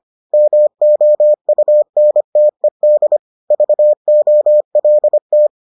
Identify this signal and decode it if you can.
Morse code, heard as a keyed tone, about 25 wpm, 600 Hz. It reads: MOUNTED VOLT